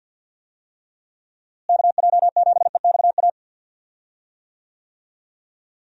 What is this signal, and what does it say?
Morse code, heard as a keyed tone, about 47 wpm, 700 Hz. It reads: KJ7EXA
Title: USA Call Signs - 6 Characters - Repeated 2x - 50wpm